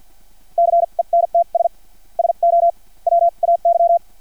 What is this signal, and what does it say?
Morse code, heard as a keyed tone, about 50 wpm, spaced as for 40 wpm, 680 Hz. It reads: CENTS SO WAY
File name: archived / CRSnet